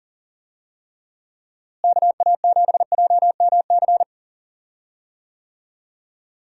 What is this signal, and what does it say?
Morse code, heard as a keyed tone, about 40 wpm, 700 Hz. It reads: KA7JMC
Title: USA Call Signs - 6 Characters - Repeated 2x - 40wpm